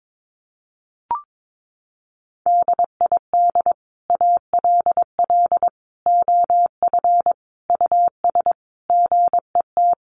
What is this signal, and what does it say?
Morse code, heard as a keyed tone, about 22 wpm, 700 Hz. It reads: DID ALL OF US GET
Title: A Sound Of Thunder 22wpm